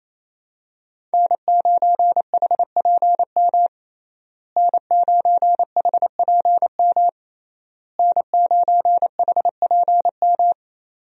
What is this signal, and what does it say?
Morse code, heard as a keyed tone, about 27 wpm, 700 Hz. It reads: N9HPM N9HPM N9HPM
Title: USA Call Signs - 5 Characters - Repeated 3x - 28wpm